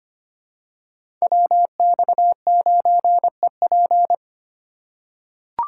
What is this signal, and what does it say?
Morse code, heard as a keyed tone, about 25 wpm, 700 Hz. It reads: WX9EP
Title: USA Call Signs - 5 Characters - 3x Character Spacing - 25wpm